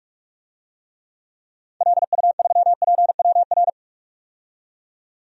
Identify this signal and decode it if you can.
Morse code, heard as a keyed tone, about 42 wpm, 700 Hz. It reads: LU3PWR
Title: DXCC-100 Call Signs - 6 Characters - 45wpm